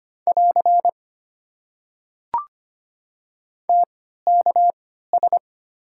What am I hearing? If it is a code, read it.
Morse code, heard as a keyed tone, about 25 wpm, spaced as for 14 wpm, 690 Hz. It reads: <AR> TKS
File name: Top 90 QSO Elements - 3x Character Spacing - 25wpm